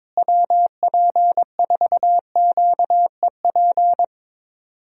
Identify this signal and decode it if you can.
Morse code, heard as a keyed tone, about 22 wpm, 700 Hz. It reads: WP4QEP
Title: USA Call Signs - 6 Characters - Repeated 3x - 22wpm